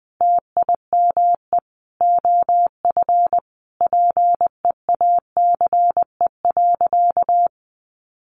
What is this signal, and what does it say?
Morse code, heard as a keyed tone, about 20 wpm, 700 Hz. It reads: TIME OF PEACE.
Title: Sentences from Top 1500 Words 20wpm